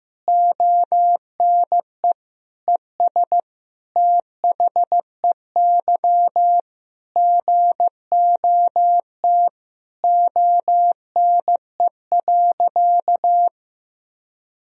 Morse code, heard as a keyed tone, about 15 wpm, 700 Hz. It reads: ONE ES THEY GOT ONE.
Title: Sentences from Top 300 Words and Abbreviations 15wpm